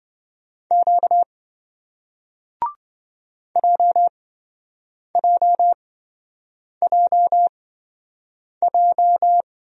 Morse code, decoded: Q JJJJ

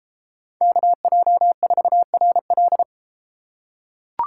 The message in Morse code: KJ4RL

